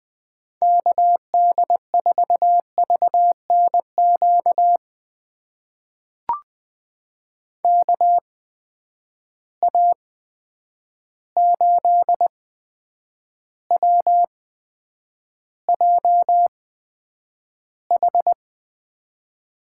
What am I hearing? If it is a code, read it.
Morse code, heard as a keyed tone, about 20 wpm, spaced as for 9 wpm, 700 Hz. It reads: KD4VNQ KA8WJH
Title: USA Call Signs - 6 Characters - 8x Character Spacing - 20wpm